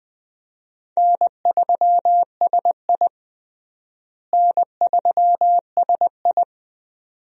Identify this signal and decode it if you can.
Morse code, heard as a keyed tone, about 21 wpm, 700 Hz. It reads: N3SI N3SI
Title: USA Call Signs - 4 Characters - Repeated 2x - 20wpm